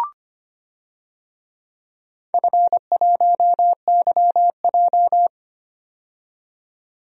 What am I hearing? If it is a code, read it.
Morse code, heard as a keyed tone, about 25 wpm, 700 Hz. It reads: F1YJ
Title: DXCC-100 Call Signs - 4 Characters - 25wpm